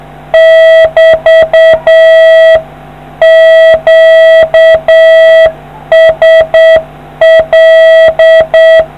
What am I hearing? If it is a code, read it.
Morse code, heard as a keyed tone, about 7 wpm, 650 Hz. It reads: <BT> QSL